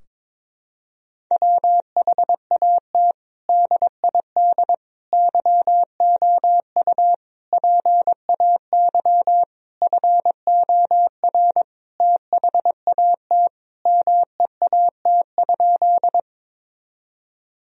Morse code, heard as a keyed tone, about 22 wpm, 700 Hz. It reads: WHAT DID YOU PAY FOR THAT MEAT?
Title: Sentences from Top 1000 Words - Review Part 3 - 22wpm